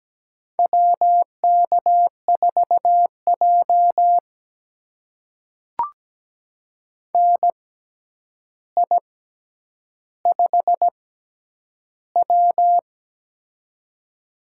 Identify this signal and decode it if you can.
Morse code, heard as a keyed tone, about 17 wpm, spaced as for 9 wpm, 700 Hz. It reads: WK4J NI5W
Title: USA Call Signs - 6x Character Spacing - 17wpm